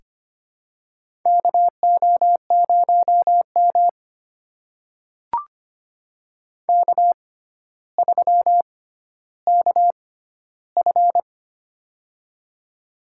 Morse code, heard as a keyed tone, about 25 wpm, spaced as for 13 wpm, 700 Hz. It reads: KO0M K3KF
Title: USA Call Signs - 4 Characters - 6x Character Spacing - 25wpm